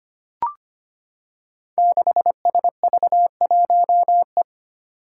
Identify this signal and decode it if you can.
Morse code, heard as a keyed tone, about 25 wpm, 700 Hz. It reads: 6SV1E
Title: Five Challenging Letter-Number Combinations 25wpm